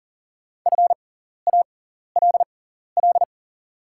Morse code, heard as a keyed tone, about 40 wpm, spaced as for 14 wpm, 690 Hz. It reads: FALL
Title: 6x Character Spacing - Top 500 Words - 40wpm